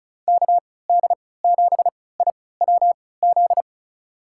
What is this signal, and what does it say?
Morse code, heard as a keyed tone, about 36 wpm, spaced as for 20 wpm, 700 Hz. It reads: KD7IWZ
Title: USA Call Signs - 6 Characters - 3x Character Spacing - 35wpm